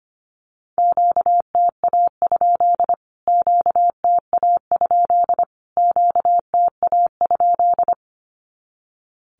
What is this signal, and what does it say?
Morse code, heard as a keyed tone, about 25 wpm, 700 Hz. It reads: QTA? QTA? QTA?
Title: Q-Codes Repeated 3x 25wpm